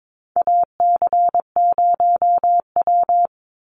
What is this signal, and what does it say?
Morse code, heard as a keyed tone, about 22 wpm, 700 Hz. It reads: AC0W
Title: Call Signs Repeated 2x - 22wpm